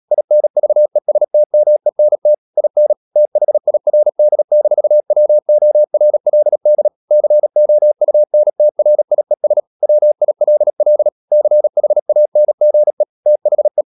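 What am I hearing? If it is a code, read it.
Morse code, heard as a keyed tone, about 37 wpm, 600 Hz. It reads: INVESTMENT IN THIRD-WORLD COUNTRIES WILL CHANGE THE
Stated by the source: Morse code practice at 35–39 wpm sending English text of Ken Follett's novel Never, from CW med Gnister.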